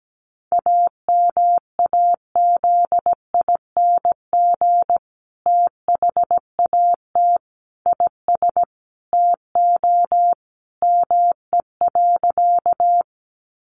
Morse code, heard as a keyed tone, about 17 wpm, 700 Hz. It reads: AMAZING THAT IS TO ME.
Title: Sentences from Top 1500 Words 17wpm